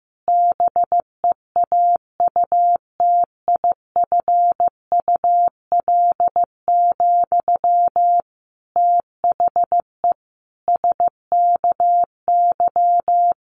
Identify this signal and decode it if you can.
Morse code, heard as a keyed tone, about 15 wpm, 700 Hz. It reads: BEAUTIFUL, THE SKY
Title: Sentences from Top 1200 Words 15wpm